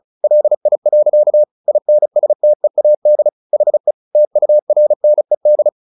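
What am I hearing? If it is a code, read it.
Morse code, heard as a keyed tone, about 34 wpm, 600 Hz. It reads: LI. INSTEAD HE TURNED
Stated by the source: Morse practice at 35 wpm from CW med Gnister, sending Ken Follett's novel Never in English.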